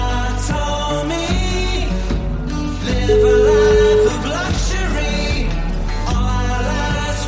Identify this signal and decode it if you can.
Ringback tone (Europe, 425 Hz), with noise.